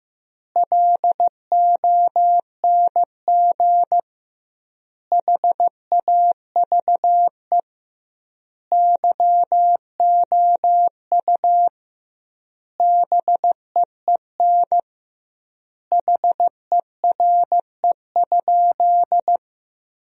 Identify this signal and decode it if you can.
Morse code, heard as a keyed tone, about 15 wpm, 700 Hz. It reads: LONG HAVE YOU BEEN HERE?
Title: 2x Word-Spacing - Sentences from Top 200 Words - 15wpm